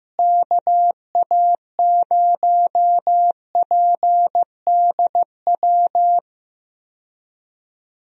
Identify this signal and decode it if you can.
Morse code, heard as a keyed tone, about 15 wpm, 700 Hz. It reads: KA0PDW
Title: USA Call Signs - 6 Characters - Repeated 3x - 15wpm